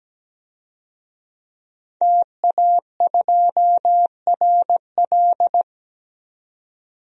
Morse code, heard as a keyed tone, about 17 wpm, 700 Hz. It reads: TA2RL
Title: DXCC-100 Call Signs - 5 Characters - 17wpm